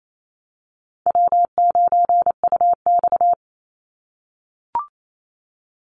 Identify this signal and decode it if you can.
Morse code, heard as a keyed tone, about 28 wpm, 700 Hz. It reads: W9UX
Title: NA SOTA Chaser Callsigns 28wpm